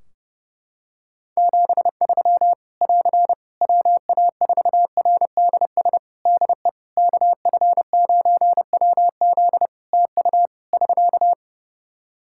Morse code, heard as a keyed tone, about 30 wpm, 700 Hz. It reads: 73 <AR> WA4RDS DE KF9WZ TU <SK>
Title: K1USN SST QSO Practice 30wpm